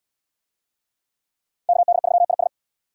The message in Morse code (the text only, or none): KD3IS